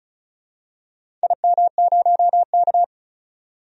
Morse code, heard as a keyed tone, about 34 wpm, 700 Hz. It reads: IM0K
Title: DXCC-100 Call Signs - 4 Characters - 35wpm